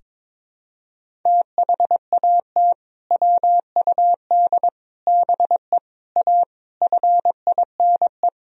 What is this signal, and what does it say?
Morse code, heard as a keyed tone, about 22 wpm, 700 Hz. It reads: THAT WUD BE A FINE
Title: Sentences from Top 500 Words and Abbreviations 22wpm